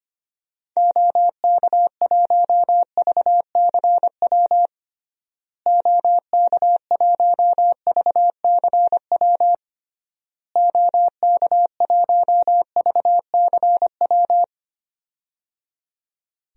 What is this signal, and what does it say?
Morse code, heard as a keyed tone, about 25 wpm, 700 Hz. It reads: OK1VCW OK1VCW OK1VCW
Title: DXCC-100 Call Signs - 6 Characters - Repeated 3x - 25wpm